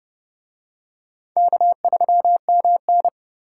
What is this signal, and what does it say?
Morse code, heard as a keyed tone, about 29 wpm, 700 Hz. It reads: K3MN